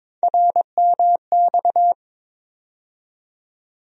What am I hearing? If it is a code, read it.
Morse code, heard as a keyed tone, about 22 wpm, 700 Hz. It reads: RMX
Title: USA Call Sign Postfixes - 3 Letters 22wpm